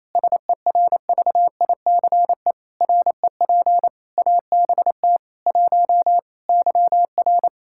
Morse code, heard as a keyed tone, about 28 wpm, 700 Hz. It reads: SERVICE REP ABT 1 YR
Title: Practice QSO - Exchange 3 - More Info - 28wpm